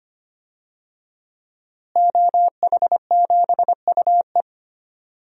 Morse code, heard as a keyed tone, about 25 wpm, 700 Hz. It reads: OH7UE
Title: DXCC-100 Call Signs - 5 Characters - Repeated 3x - 25wpm